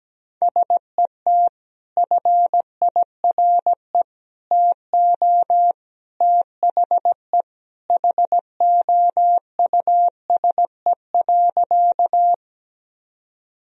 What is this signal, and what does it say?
Morse code, heard as a keyed tone, about 17 wpm, 700 Hz. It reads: SET FIRE TO THE HOUSE.
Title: Sentences from Top 1000 Words - Review Part 1 - 17wpm